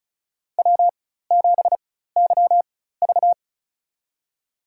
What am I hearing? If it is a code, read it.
Morse code, heard as a keyed tone, about 36 wpm, spaced as for 16 wpm, 700 Hz. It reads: W7YV
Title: USA Call Signs - 4 Characters - 4x Character Spacing - 35wpm